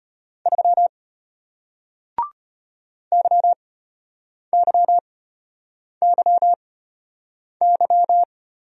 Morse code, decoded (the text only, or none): 3 YYYY